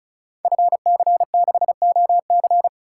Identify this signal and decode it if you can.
Morse code, heard as a keyed tone, about 36 wpm, 700 Hz. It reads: FC6OC
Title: Code Groups - 5 Letter:Numbers - 35wpm